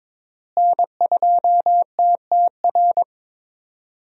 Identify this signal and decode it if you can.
Morse code, heard as a keyed tone, about 21 wpm, 700 Hz. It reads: N2TTR
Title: USA Call Signs - 5 Characters - Repeated 2x - 22wpm